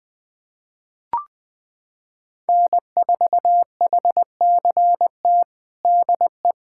Morse code, heard as a keyed tone, about 20 wpm, 700 Hz. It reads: N4HCT DE
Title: USA Callsign DE Callsign - 5 Characters - 20wpm